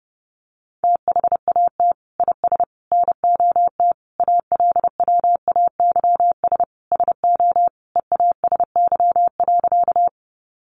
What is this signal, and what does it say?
Morse code, heard as a keyed tone, about 30 wpm, 700 Hz. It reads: THAT IS NOT ALWAYS SO EASY.
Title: Sentences from Top 700 Words 30wpm